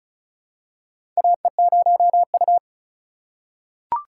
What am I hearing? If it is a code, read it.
Morse code, heard as a keyed tone, about 34 wpm, 700 Hz. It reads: AE0U
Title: USA Call Signs - 4 Characters - 4x Character Spacing - 35wpm